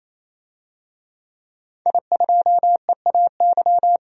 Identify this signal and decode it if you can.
Morse code, heard as a keyed tone, about 27 wpm, 690 Hz. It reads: I2EAY
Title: DXCC-100 Call Signs - 5 Characters - 28wpm